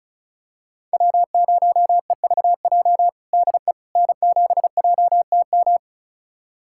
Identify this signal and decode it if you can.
Morse code, heard as a keyed tone, about 34 wpm, 700 Hz. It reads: W0EVJ DE N7JTM
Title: USA Callsign DE Callsign - 5 Characters - 35wpm